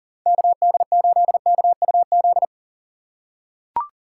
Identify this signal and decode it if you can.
Morse code, heard as a keyed tone, about 40 wpm, 700 Hz. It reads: KD8KUZ
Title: USA Call Signs - 6 Characters - 2x Character Spacing - 40wpm